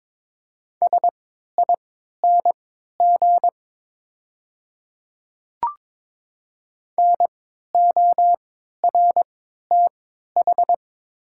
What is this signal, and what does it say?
Morse code, heard as a keyed tone, about 22 wpm, spaced as for 13 wpm, 690 Hz. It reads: SING NORTH
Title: 3x Character Spacing - Top 500 Words - 22wpm